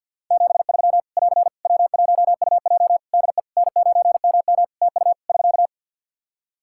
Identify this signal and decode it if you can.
Morse code, heard as a keyed tone, about 51 wpm, 700 Hz. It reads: 73 <AR> W1UJ DE N9MM TU <SK>
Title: Practice QSO - Exchange 4 - Ending - 50wpm